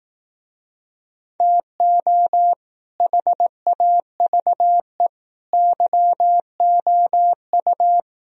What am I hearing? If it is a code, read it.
Morse code, heard as a keyed tone, about 18 wpm, 700 Hz. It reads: TO HAVE YOU
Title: Speed-Racing - Sets of 5 Words 15wpm